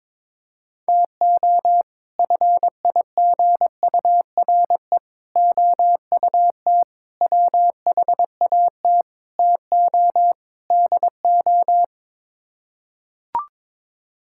Morse code, heard as a keyed tone, about 22 wpm, 700 Hz. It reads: TO FIGURE OUT WHAT TO DO
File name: practice / 22wpm / Sets of 6 Words - 22wpm